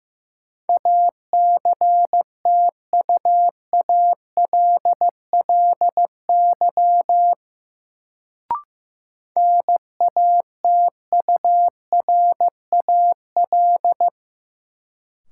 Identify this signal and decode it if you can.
Morse code, heard as a keyed tone, about 15 wpm, 700 Hz. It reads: ACTUALLY NATURAL
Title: Sentences from Top 700 Words - New Word Drill - 15wpm